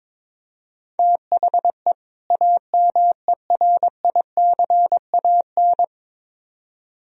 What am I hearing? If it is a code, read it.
Morse code, heard as a keyed tone, about 22 wpm, 700 Hz. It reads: THE AMERICAN
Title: Speed-Racing - Sets of 2 Words 22wpm